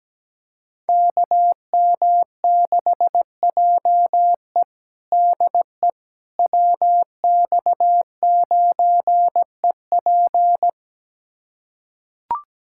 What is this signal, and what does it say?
Morse code, heard as a keyed tone, about 17 wpm, 700 Hz. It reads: KM6JE DE WX9EP
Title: USA Callsign DE Callsign 17wpm